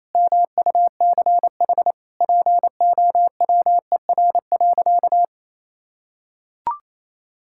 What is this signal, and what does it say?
Morse code, heard as a keyed tone, about 28 wpm, 700 Hz. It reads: MUCH POWER.
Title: Sentences from Top 1000 Words - Review Part 5 - 28wpm